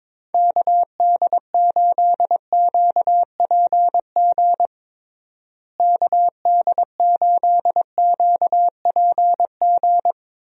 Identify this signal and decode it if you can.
Morse code, heard as a keyed tone, about 22 wpm, 700 Hz. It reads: KD8QPG KD8QPG
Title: USA Call Signs - 6 Characters - Repeated 3x - 22wpm